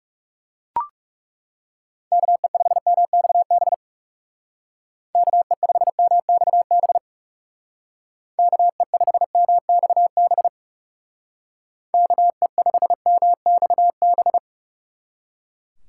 Morse code, heard as a keyed tone, about 36 wpm, 700 Hz. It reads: KE5MXB KE5MXB KE5MXB KE5MXB
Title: Speed Racing - USA 2x3 Call Signs 30wpm